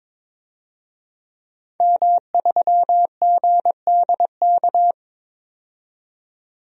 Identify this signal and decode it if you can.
Morse code, heard as a keyed tone, about 22 wpm, 700 Hz. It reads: M3GDK
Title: DXCC-100 Call Signs - 5 Characters - 22wpm